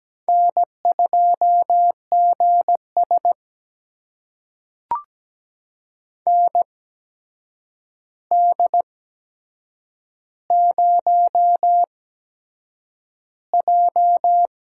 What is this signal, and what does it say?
Morse code, heard as a keyed tone, about 17 wpm, spaced as for 7 wpm, 700 Hz. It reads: N2GS ND0J
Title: USA Call Signs - 4 Characters - 8x Character Spacing - 17wpm